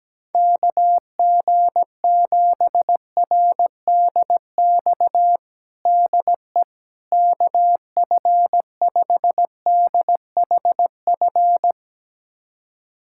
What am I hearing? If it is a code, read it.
Morse code, heard as a keyed tone, about 17 wpm, 700 Hz. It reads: KG7RDX DE KF5DHF
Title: USA Callsign DE Callsign - 6 Characters - 17wpm